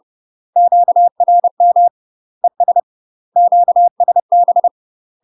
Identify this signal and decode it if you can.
Morse code, heard as a keyed tone, about 30 wpm, 700 Hz. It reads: QRM ES QSB
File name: UK MASTER QSOs TRAINING / 30WPM / UK MASTER QSOs 0047